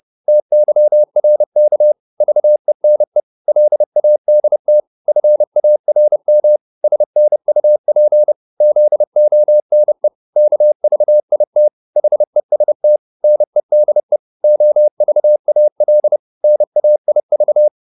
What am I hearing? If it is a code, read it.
Morse code, heard as a keyed tone, about 30 wpm, 600 Hz. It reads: TYRK VENE LADT FARM SNUP ZONE KVIT HEST NEDE OVAL NAIV
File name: Korte ord DK 30wpm